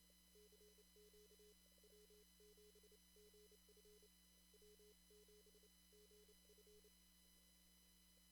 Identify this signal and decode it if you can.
Morse code, heard as a keyed tone, about 28 wpm, spaced as for 25 wpm, 420 Hz. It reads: CQ W7GF W7GF
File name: CW / kt5j wpx cw-op-k5pi